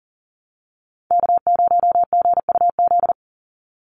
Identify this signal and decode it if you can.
Morse code, heard as a keyed tone, about 37 wpm, 700 Hz. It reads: K0GUZ
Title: Call Signs Repeated 3x - 40wpm